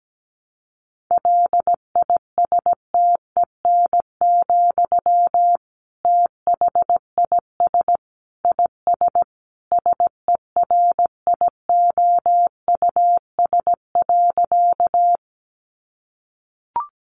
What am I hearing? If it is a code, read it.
Morse code, heard as a keyed tone, about 17 wpm, 700 Hz. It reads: LISTEN, THIS IS SERIOUS.
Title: Sentences from Top 1100 Words 17wpm